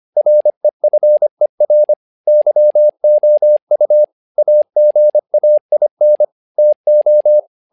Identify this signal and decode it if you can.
Morse code, heard as a keyed tone, about 25 wpm, 600 Hz. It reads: REFER YOU AGAIN TO
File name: learning center / tower trouble / tower trouble 25wpm